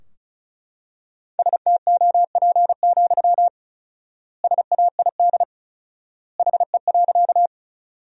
Morse code, heard as a keyed tone, about 36 wpm, 700 Hz. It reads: STOP, SAID HE.
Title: 4x Word-Spacing - Sentences from Top 300 Words - 35wpm